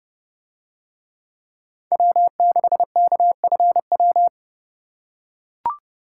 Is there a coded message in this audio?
Yes: Morse code from 1.9 to 4.3 s, heard as a keyed tone, about 29 wpm, 700 Hz. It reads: W6KFW